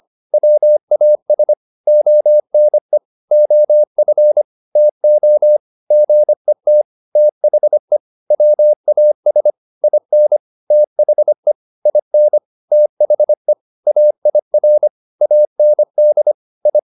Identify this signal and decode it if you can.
Morse code, heard as a keyed tone, about 25 wpm, 600 Hz. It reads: WAS ONE OF TO GET THE WAS IN THE IN THE AIR AND I